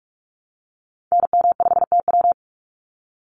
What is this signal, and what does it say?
Morse code, heard as a keyed tone, about 44 wpm, 700 Hz. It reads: NM5TW
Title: NA SOTA Activator Callsigns 45wpm